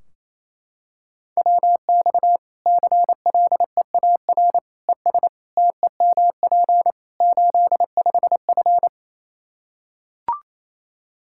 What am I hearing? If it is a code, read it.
Morse code, heard as a keyed tone, about 28 wpm, 700 Hz. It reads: WX CLEAR ES TEMP 85F
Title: Practice QSO - Exchange 2 - Info - 28wpm